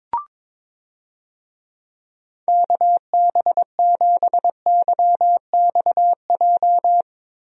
Morse code, heard as a keyed tone, about 22 wpm, 700 Hz. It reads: KB7YXJ
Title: USA Call Signs - 6 Characters - Repeated 2x - 22wpm